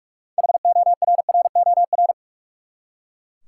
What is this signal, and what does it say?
Morse code, heard as a keyed tone, about 41 wpm, 700 Hz. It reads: HORROR
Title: Six Letter Words - 45wpm